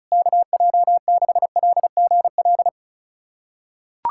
Morse code, heard as a keyed tone, about 36 wpm, 700 Hz. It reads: KJ6LGL
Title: USA Callsign DE Callsign - 6 Characters - 35wpm